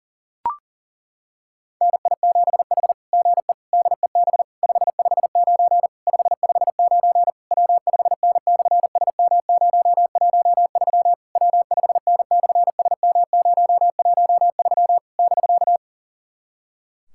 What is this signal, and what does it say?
Morse code, heard as a keyed tone, about 40 wpm, 700 Hz. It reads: NI7H GE DEB 559 559 W5N/SM013 W5N/SM013 <BK>
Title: SOTA QSO Practice 40wpm